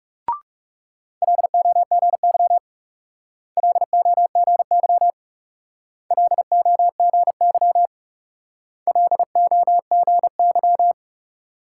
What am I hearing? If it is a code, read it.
Morse code, heard as a keyed tone, about 40 wpm, 700 Hz. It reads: LOGY LOGY LOGY LOGY